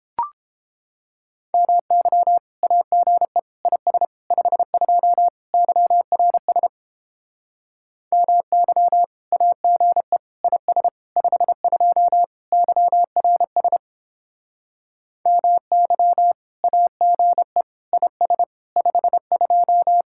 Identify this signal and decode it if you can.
Morse code, heard as a keyed tone, about 29 wpm, 700 Hz. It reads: MY AGE IS 52 YRS MY AGE IS 52 YRS MY AGE IS 52